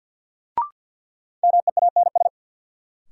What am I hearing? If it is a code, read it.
Morse code, heard as a keyed tone, about 47 wpm, 700 Hz. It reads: MEANS
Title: Sets of 5 Words - New Word Drill - 50wpm